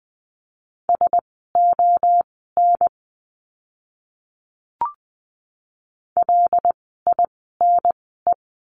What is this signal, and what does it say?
Morse code, heard as a keyed tone, about 20 wpm, spaced as for 15 wpm, 700 Hz. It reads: SON LINE
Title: 094 Lesson 16f - Letter c - Words 20wpm